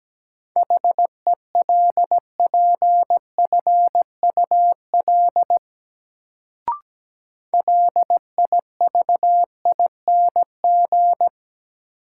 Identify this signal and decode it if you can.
Morse code, heard as a keyed tone, about 17 wpm, 700 Hz. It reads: HELPFUL LIVING
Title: Sentences from Top 1000 Words - Review Part 6 - New Word Drill - 17wpm